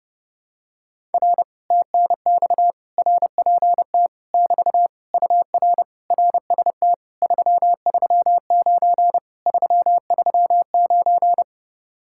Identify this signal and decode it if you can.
Morse code, heard as a keyed tone, about 30 wpm, 700 Hz. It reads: R TNX RPT <BT> UR RST 339 339
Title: Speed-Racing Common QSO Phrases - 30wpm